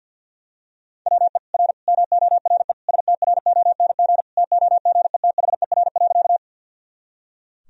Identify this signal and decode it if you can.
Morse code, heard as a keyed tone, about 45 wpm, 700 Hz. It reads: WE R MORE STRONG TOGETHER.
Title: Sentences from Top 500 Words and Abbreviations 50wpm